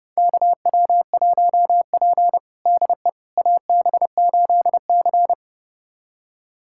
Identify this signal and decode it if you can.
Morse code, heard as a keyed tone, about 30 wpm, 700 Hz. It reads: KW1P DE AB8C
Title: USA Callsign DE Callsign - 4 Characters - 30wpm